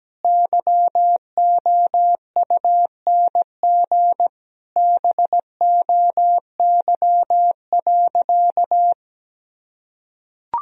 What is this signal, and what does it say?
Morse code, heard as a keyed tone, about 17 wpm, 700 Hz. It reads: YOUNG BOY.
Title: Sentences from Top 1000 Words - Review Part 4 - 17wpm